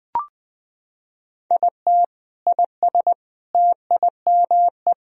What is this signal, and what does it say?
Morse code, heard as a keyed tone, about 20 wpm, 700 Hz. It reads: IT IS TIME
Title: Sentences from Top 500 Words and Abbreviations 20wpm